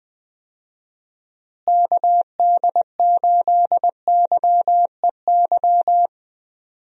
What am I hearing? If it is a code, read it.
Morse code, heard as a keyed tone, about 20 wpm, 700 Hz. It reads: KD8YEY
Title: USA Call Signs - 6 Characters - Repeated 2x - 20wpm